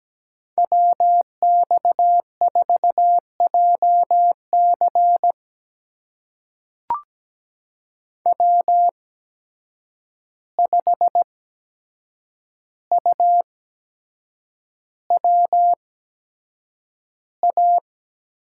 Morse code, heard as a keyed tone, about 17 wpm, spaced as for 7 wpm, 700 Hz. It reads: WX4JC W5UWA
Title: USA Call Signs - 5 Characters - 8x Character Spacing - 17wpm